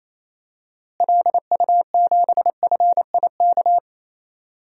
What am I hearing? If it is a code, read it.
Morse code, heard as a keyed tone, about 28 wpm, 700 Hz. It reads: LU7FIK
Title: DXCC-100 Call Signs - 6 Characters - Repeated 2x - 28wpm